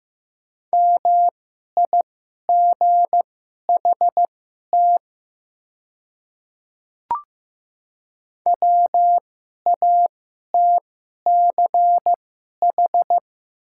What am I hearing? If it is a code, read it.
Morse code, heard as a keyed tone, about 15 wpm, spaced as for 11 wpm, 700 Hz. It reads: MIGHT WATCH
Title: 2x Character Spacing - Top 500 Words - 15wpm